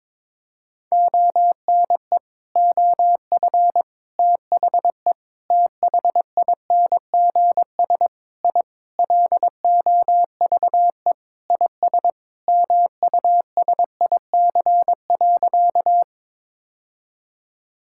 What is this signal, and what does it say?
Morse code, heard as a keyed tone, about 22 wpm, 700 Hz. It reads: ONE OF THE THINGS I LOVE IS MUSIC.
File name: practice / 22wpm / Sentences with 4-Word Phrases - 22wpm